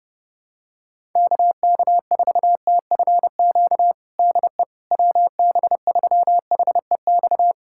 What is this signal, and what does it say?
Morse code, heard as a keyed tone, about 31 wpm, 700 Hz. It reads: KK4TFQ DE WB3HEX
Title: USA Callsign DE Callsign - 6 Characters - 30wpm